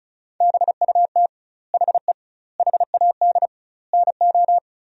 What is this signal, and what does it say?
Morse code, heard as a keyed tone, about 36 wpm, 700 Hz. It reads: BUT HE HAD NO
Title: 2x Word-Spacing - Sentences from Top 100 Words - 35wpm